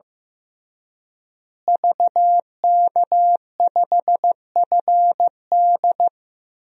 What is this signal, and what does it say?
Morse code, heard as a keyed tone, about 15 wpm, 700 Hz. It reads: VK5FD
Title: DXCC-100 Call Signs - 5 Characters - Repeated 3x - 15wpm